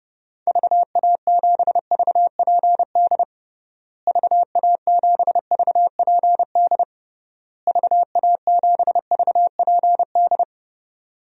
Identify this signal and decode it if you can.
Morse code, heard as a keyed tone, about 32 wpm, 700 Hz. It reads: VA7VPD VA7VPD VA7VPD
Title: DXCC-100 Call Signs - 6 Characters - Repeated 3x - 30wpm